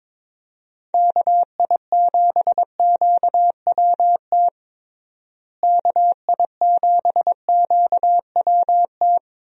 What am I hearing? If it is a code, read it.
Morse code, heard as a keyed tone, about 22 wpm, 700 Hz. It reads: KI7QWT KI7QWT
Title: USA Call Signs - 6 Characters - Repeated 2x - 22wpm